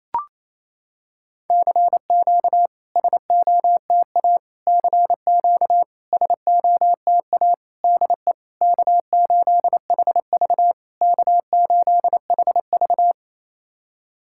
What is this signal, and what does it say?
Morse code, heard as a keyed tone, about 28 wpm, 700 Hz. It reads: CQ SOTA CQ SOTA DE K8HV K8HV
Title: SOTA QSO Practice 28wpm